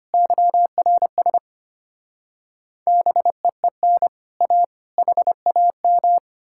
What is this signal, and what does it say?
Morse code, heard as a keyed tone, about 33 wpm, 700 Hz. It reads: YRS BEEN A HAM